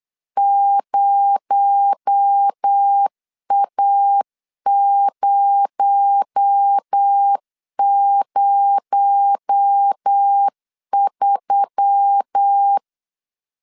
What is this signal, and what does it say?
Morse code, heard as a keyed tone, about 8 wpm, 800 Hz. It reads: 0A003
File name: LEZIONI / 6 PKX0 / 6 PKX0 8wpm 3341